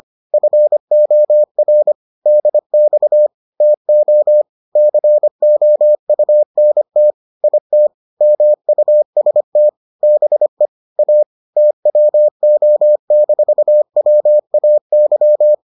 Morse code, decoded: FOR DX TO COUNT IT MUST BE A TWO-WAY

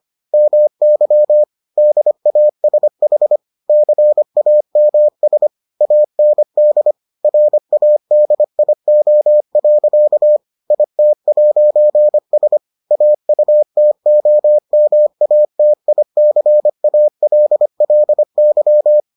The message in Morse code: MY DASH CAMS AND RADIO. IT'S AUTOMATICALLY

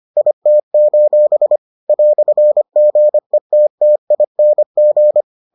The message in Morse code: IT: "GETTING

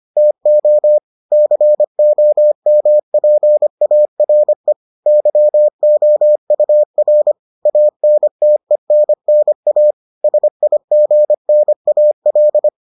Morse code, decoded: TO COMPARE YOUR ANTENNA SIGNAL

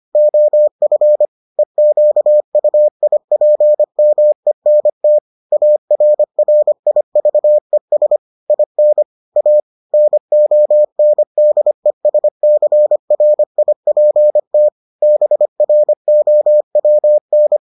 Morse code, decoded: OF EQUIPMENT ARRIVES IN A NONDESCRIPT BROWN

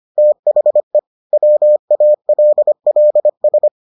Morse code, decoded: THE WALLS

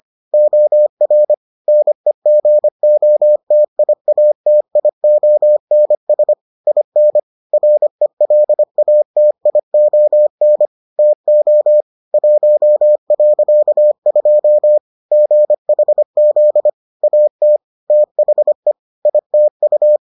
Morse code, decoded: OR NEGOTIATIONS IN RELATION TO 1.2 GHZ AT THE ITU

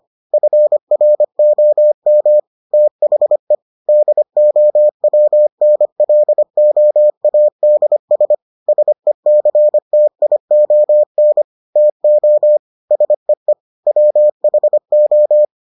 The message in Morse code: FROM THE DOWNLOADS SECTION TO SEE WHO

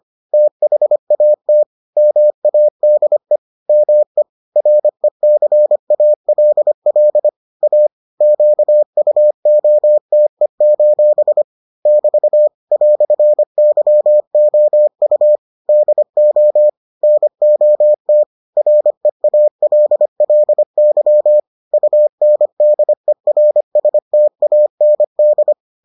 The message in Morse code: THAT MADE ME RECALL A QUOTE: <BT> "YOU DO NOT REALLY UNDERSTAND

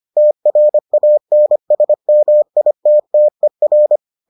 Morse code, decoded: TRANSMITTER

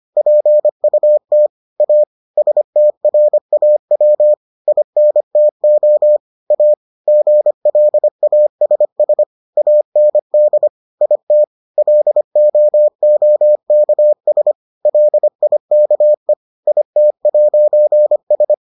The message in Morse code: PUT A STRAW INTO A GLASS AND IT LOOKS LIKE IT'S

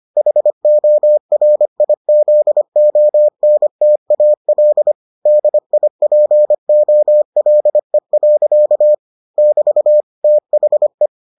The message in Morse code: HORIZONTAL DIPOLE. <BT> THE

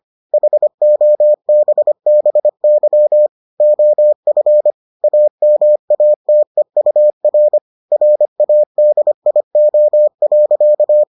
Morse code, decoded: HOBBY OF AMATEUR RADIO.

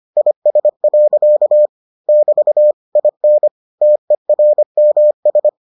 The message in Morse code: IS. <BT> IN TERMS

